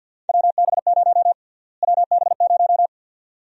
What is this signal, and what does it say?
Morse code, heard as a keyed tone, about 52 wpm, 700 Hz. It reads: WB0 WB0
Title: USA Call Sign Prefixes - Repeated 2x - 50wpm